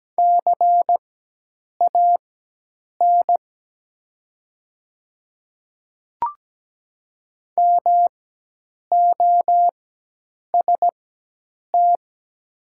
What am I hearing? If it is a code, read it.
Morse code, heard as a keyed tone, about 17 wpm, spaced as for 8 wpm, 700 Hz. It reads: CAN MOST
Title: 4x Character Spacing - Top 200 Words - 17wpm